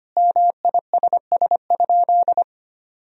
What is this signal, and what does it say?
Morse code, heard as a keyed tone, about 25 wpm, 700 Hz. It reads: MISS?
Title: Sentences from Top 500 Words and Abbreviations 25wpm